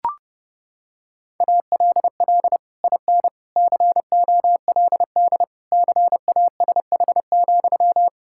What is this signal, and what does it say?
Morse code, heard as a keyed tone, about 30 wpm, 700 Hz. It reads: ALL IN COLD CASH,
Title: Sentences from Top 1000 Words - Review Part 5 - 30wpm